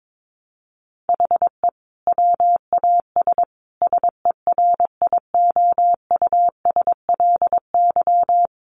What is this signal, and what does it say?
Morse code, heard as a keyed tone, about 22 wpm, 700 Hz. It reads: HE WAS SERIOUSLY
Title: Sentences from Top 1200 Words 22wpm